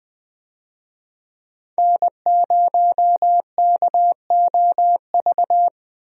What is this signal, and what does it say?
Morse code, heard as a keyed tone, about 20 wpm, 700 Hz. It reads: N0KOV
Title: USA Call Signs - 5 Characters - Repeated 3x - 20wpm